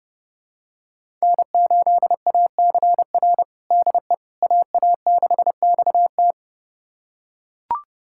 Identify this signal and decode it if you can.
Morse code, heard as a keyed tone, about 32 wpm, 700 Hz. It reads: N8ACR DE AA6XT
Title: USA Callsign DE Callsign 30wpm